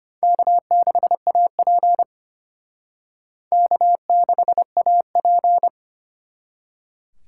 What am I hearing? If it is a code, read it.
Morse code, heard as a keyed tone, about 33 wpm, 700 Hz. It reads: K6AP K6AP